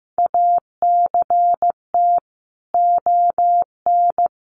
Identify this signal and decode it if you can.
Morse code, heard as a keyed tone, about 15 wpm, 700 Hz. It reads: ACT ON
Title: Sentences from Top 1400 Words 15wpm